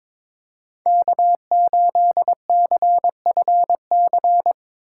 Morse code, heard as a keyed tone, about 21 wpm, 700 Hz. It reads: K8CFC
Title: USA Call Signs - 5 Characters - 3x Character Spacing - 22wpm